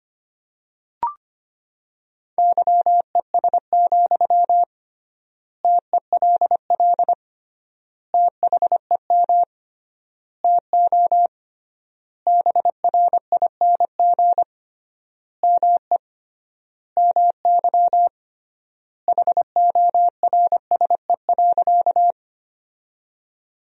Morse code, heard as a keyed tone, about 25 wpm, 700 Hz. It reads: YES, TELL THEM TO BRING ME MY HORSE.
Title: 3x Word-Spacing - Sentences from Top 500 Words - 25wpm